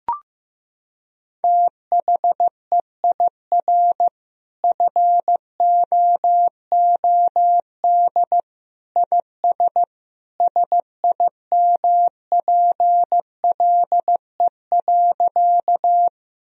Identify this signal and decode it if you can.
Morse code, heard as a keyed tone, about 15 wpm, 700 Hz. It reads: THEIR FOOD IS SIMPLE.
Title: Sentences from Top 1000 Words - Review Part 1 - 15wpm